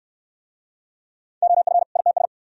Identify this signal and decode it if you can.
Morse code, heard as a keyed tone, about 62 wpm, 690 Hz. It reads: OF ITS